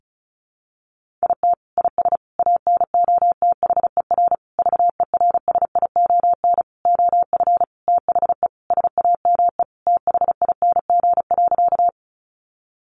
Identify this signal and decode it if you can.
Morse code, heard as a keyed tone, about 34 wpm, 700 Hz. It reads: IT IS ANOTHER VERSION OF THE SAME THING.
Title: Sentences from Top 1400 Words 35wpm